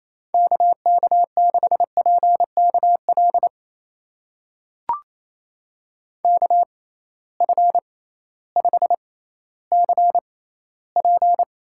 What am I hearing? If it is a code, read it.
Morse code, heard as a keyed tone, about 28 wpm, spaced as for 15 wpm, 700 Hz. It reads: KK6PKL KF5CP
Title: USA Call Signs - 6 Characters - 6x Character Spacing - 28wpm